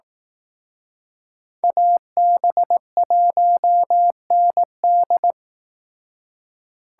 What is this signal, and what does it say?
Morse code, heard as a keyed tone, about 18 wpm, 700 Hz. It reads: AB1ND
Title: Speed Racing - USA 2x2 Call Signs 15wpm